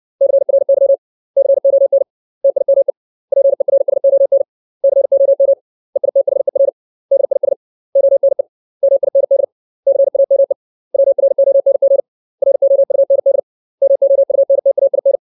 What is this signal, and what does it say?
Morse code, heard as a keyed tone, about 60 wpm, 550 Hz. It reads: CRY CON TIME PERSON COM EITHER DIS ONE MIND CAME WRONG ROUND MOUNTAIN
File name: 632 words / 632 Words 60WPM